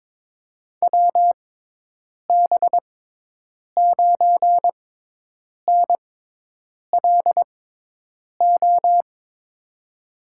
Morse code, heard as a keyed tone, about 22 wpm, spaced as for 8 wpm, 700 Hz. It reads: WB9NLO
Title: USA Call Signs - 6 Characters - 6x Character Spacing - 22wpm